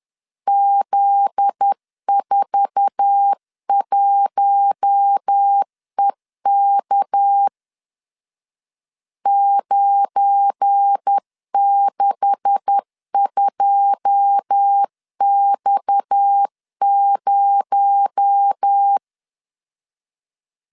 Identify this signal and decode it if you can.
Morse code, heard as a keyed tone, about 10 wpm, 800 Hz. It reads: Z41EK 962X0
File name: LEZIONI / 9 286 10wpm 2735